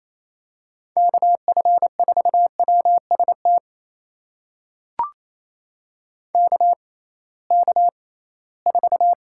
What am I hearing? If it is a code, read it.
Morse code, heard as a keyed tone, about 28 wpm, spaced as for 18 wpm, 700 Hz. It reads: KF4WST KK4
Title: USA Call Signs - 6 Characters - 6x Character Spacing - 28wpm